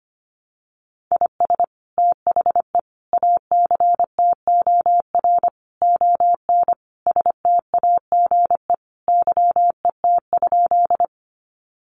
Morse code, heard as a keyed tone, about 25 wpm, 700 Hz. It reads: IS THE ACTOR ON STAGE YET?
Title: Sentences from Top 1500 Words 25wpm